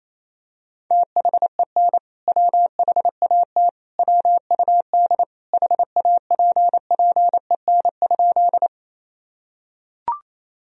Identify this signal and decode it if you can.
Morse code, heard as a keyed tone, about 28 wpm, 700 Hz. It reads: THEN WHAT WUD HAPPEN?
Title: Sentences from Top 400 Words and Abbreviations 28wpm